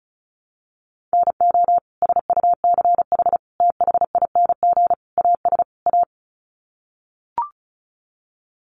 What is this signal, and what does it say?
Morse code, heard as a keyed tone, about 34 wpm, 700 Hz. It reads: NO SUCH THING AS A
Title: Sets of 5 Words Encore 35wpm